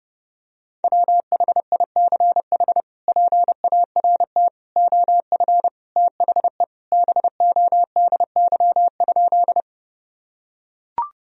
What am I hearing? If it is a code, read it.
Morse code, heard as a keyed tone, about 30 wpm, 700 Hz. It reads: WHICH PART OF THE BODY?
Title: Sentences from Top 1000 Words - Review Part 2 - 30wpm